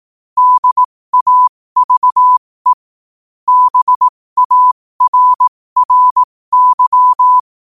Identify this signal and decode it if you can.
Morse code, heard as a keyed tone, about 18 wpm, spaced as for 15 wpm, 1000 Hz. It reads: DAVE BARRY
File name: qotdfiles / 15wpm